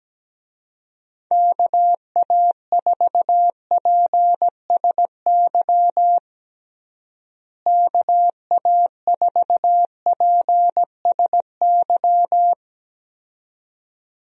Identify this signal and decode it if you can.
Morse code, heard as a keyed tone, about 17 wpm, 700 Hz. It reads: KA4PSY KA4PSY
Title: USA Call Signs - 6 Characters - Repeated 2x - 17wpm